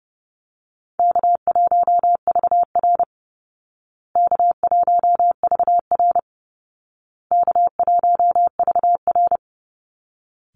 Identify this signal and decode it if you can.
Morse code, heard as a keyed tone, about 30 wpm, 700 Hz. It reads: K1VR K1VR K1VR
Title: Call Signs Repeated 2x - 30wpm